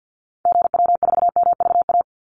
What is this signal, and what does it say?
Morse code, heard as a keyed tone, about 50 wpm, 700 Hz. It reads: GW4MVA
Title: Call Signs - 50wpm